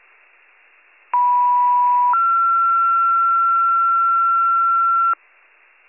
Two-tone page, heard as a one-second tone, then a three-second tone, about 980 then 1360 Hz, with noise. It is heard from 1.1 to 5.1 s.